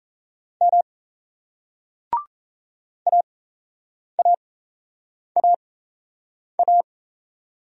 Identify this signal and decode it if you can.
Morse code, heard as a keyed tone, about 35 wpm, spaced as for 8 wpm, 700 Hz. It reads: M AAAA
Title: Speed-Racing Single Letters 28wpm